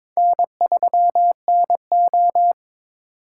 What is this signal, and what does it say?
Morse code, heard as a keyed tone, about 22 wpm, 700 Hz. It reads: N3NO
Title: USA Call Signs - 4 Characters - 3x Character Spacing - 22wpm